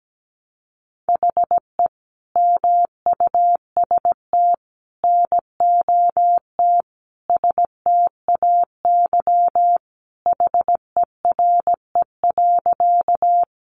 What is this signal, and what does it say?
Morse code, heard as a keyed tone, about 17 wpm, 700 Hz. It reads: HE MUST NOT STAY HERE.
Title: Sentences from Top 600 Words 17wpm